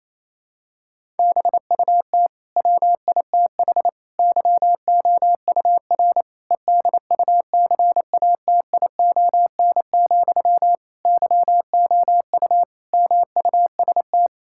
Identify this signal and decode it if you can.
Morse code, heard as a keyed tone, about 28 wpm, 700 Hz. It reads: BUT WITH YOUR EDUCATION, YOU MUST
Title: Sentences from Top 1000 Words - Review Part 6 - 28wpm